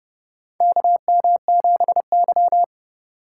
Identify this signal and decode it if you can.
Morse code, heard as a keyed tone, about 30 wpm, 700 Hz. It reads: KM7Y
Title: USA Call Signs - 4 Characters - 4x Character Spacing - 30wpm